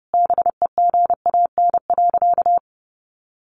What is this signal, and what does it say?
Morse code, heard as a keyed tone, about 30 wpm, 700 Hz. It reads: BEGAN.
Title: Sentences from Top 1100 Words 30wpm